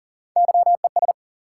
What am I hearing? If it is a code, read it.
Morse code, heard as a keyed tone, about 40 wpm, 700 Hz. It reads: YES
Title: Three Letter Words - 40wpm